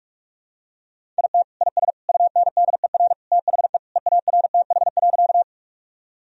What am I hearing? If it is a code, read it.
Morse code, heard as a keyed tone, about 42 wpm, 700 Hz. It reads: IT IS UNDER THE EARTH.
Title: Sentences from Top 1000 Words - Review Part 1 - 45wpm